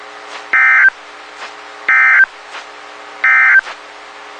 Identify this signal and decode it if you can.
Emergency Alert System (SAME) tones, heard as harsh buzzing bursts. It is heard from 0.4 to 3.7 s.
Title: WXK91 NOAA Weather Radio